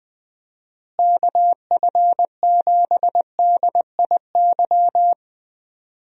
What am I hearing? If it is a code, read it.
Morse code, heard as a keyed tone, about 20 wpm, 700 Hz. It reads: KF7DIY
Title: USA Call Signs - 6 Characters - Repeated 2x - 20wpm